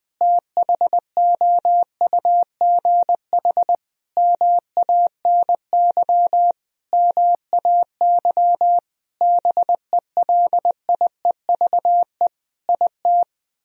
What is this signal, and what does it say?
Morse code, heard as a keyed tone, about 20 wpm, 700 Hz. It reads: THOUGH MANY MAY BELIEVE IT
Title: Sentences from Top 1000 Words - Review Part 4 - 20wpm